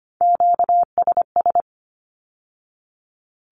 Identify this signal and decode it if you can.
Morse code, heard as a keyed tone, about 25 wpm, 700 Hz. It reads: QSS
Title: Q-Codes with Spoken Meaning - 25wpm